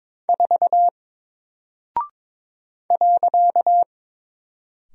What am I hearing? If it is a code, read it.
Morse code, heard as a keyed tone, about 22 wpm, 700 Hz. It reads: 4 .
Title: Single Character (Letter:Number:Punctuation:Prosign) Alt 22wpm